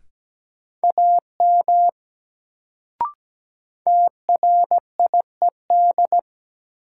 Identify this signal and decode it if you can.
Morse code, heard as a keyed tone, about 17 wpm, 700 Hz. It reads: AM TRIED
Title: Sets of 2 Words - New Word Drill - 17wpm